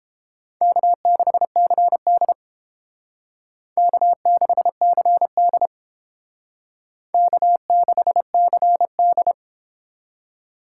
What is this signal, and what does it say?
Morse code, heard as a keyed tone, about 30 wpm, 700 Hz. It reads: K6CD K6CD K6CD